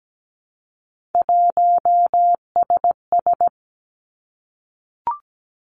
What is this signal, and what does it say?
Morse code, heard as a keyed tone, about 17 wpm, 700 Hz. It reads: 1SS
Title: Three Challenging Letter-Number Combinations 17wpm